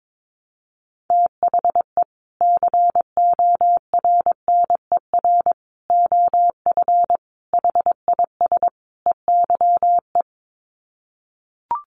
Morse code, heard as a keyed tone, about 22 wpm, 700 Hz. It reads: THE CORNER OF HIS EYE
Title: Sets of 5 Words Encore2 22wpm